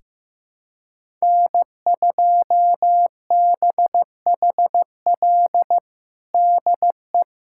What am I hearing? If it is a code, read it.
Morse code, heard as a keyed tone, about 15 wpm, 700 Hz. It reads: N2BHL DE
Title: USA Callsign DE Callsign - 5 Characters - 15wpm